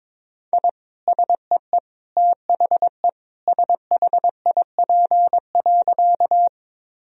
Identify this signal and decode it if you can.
Morse code, heard as a keyed tone, about 22 wpm, 700 Hz. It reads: I SEE THE SHIP.
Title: Sentences from Top 1000 Words - Review Part 2 - 22wpm